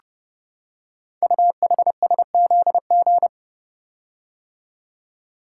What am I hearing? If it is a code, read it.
Morse code, heard as a keyed tone, about 29 wpm, 700 Hz. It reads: UHSZG
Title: Code Groups - 5 Letters - 30wpm